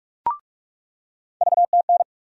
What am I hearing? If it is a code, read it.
Morse code, heard as a keyed tone, about 46 wpm, 700 Hz. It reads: VTN
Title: USA Call Sign Postfixes - 3 Letters 45wpm